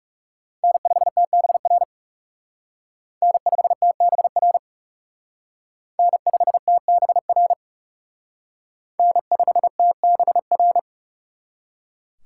Morse code, heard as a keyed tone, about 37 wpm, 700 Hz. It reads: N5TBR N5TBR N5TBR N5TBR
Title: Speed Racing - USA 1x3 Call Signs 30wpm